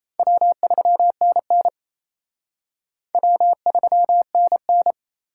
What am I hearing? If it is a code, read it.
Morse code, heard as a keyed tone, about 34 wpm, 700 Hz. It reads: W3NN W3NN
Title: Speed Racing - USA 1x2 Call Signs 28wpm